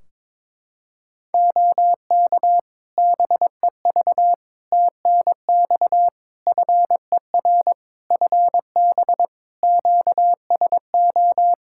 Morse code, heard as a keyed tone, about 22 wpm, 700 Hz. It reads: OK BEV TNX FER FB QSO
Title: Practice QSO - Exchange 4 - Ending - 22wpm